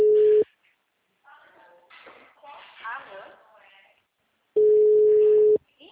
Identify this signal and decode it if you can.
Ringback tone (Europe, 425 Hz), with speech.
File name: out- -2011- - - .87354